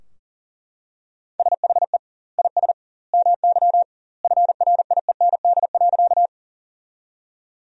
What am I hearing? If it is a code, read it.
Morse code, heard as a keyed tone, about 40 wpm, 700 Hz. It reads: SHE IS MY FRIEND.
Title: 2x Word-Spacing - Sentences from Top 300 Words - 40wpm